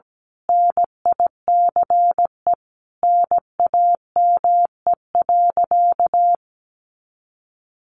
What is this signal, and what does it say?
Morse code, heard as a keyed tone, about 17 wpm, 700 Hz. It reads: NICE NAME.
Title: Sentences from Top 1100 Words 17wpm